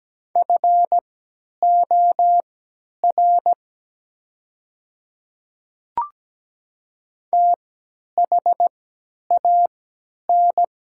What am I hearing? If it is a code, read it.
Morse code, heard as a keyed tone, about 17 wpm, spaced as for 10 wpm, 700 Hz. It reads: FOR THAN
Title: 3x Character Spacing - Top 200 Words - 17wpm